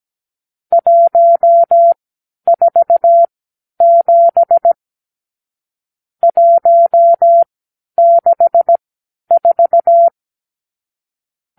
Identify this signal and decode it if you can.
Morse code, heard as a keyed tone, about 17 wpm, spaced as for 10 wpm, 680 Hz. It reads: 147 164
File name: k7qo / Lesson 54 (9)